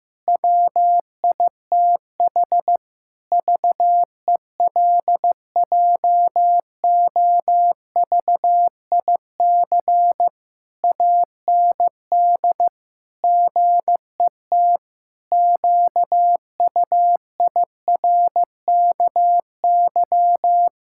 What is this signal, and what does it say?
Morse code, heard as a keyed tone, about 15 wpm, 700 Hz. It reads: WITH VELJOVIC AND GET QUIRKY
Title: Pangram Challenge v2 - 15wpm